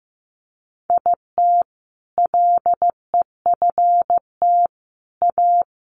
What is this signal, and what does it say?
Morse code, heard as a keyed tone, about 15 wpm, 700 Hz. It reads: IT LEFT A
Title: Sentences from Top 1500 Words 15wpm